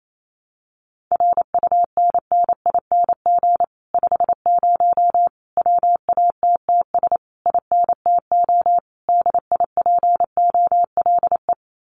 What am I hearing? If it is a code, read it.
Morse code, heard as a keyed tone, about 28 wpm, 700 Hz. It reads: RUNNING 50 WATTS INTO DIPOLE
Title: Common QSO Phrases Encore 28wpm